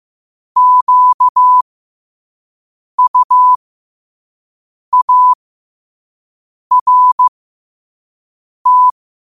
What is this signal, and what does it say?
Morse code, heard as a keyed tone, about 15 wpm, spaced as for 5 wpm, 1000 Hz. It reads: QUART
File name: qotdfiles / 5wpm